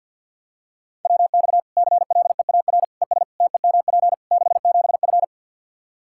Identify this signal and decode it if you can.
Morse code, heard as a keyed tone, about 51 wpm, 700 Hz. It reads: WX CLEAR ES TEMP 67F
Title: K1USN SST QSO Practice 50wpm